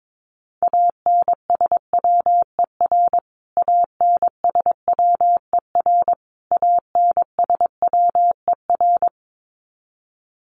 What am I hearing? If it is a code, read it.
Morse code, heard as a keyed tone, about 22 wpm, 700 Hz. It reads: ANSWER ANSWER ANSWER
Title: Top 300 Words Repeated 22wpm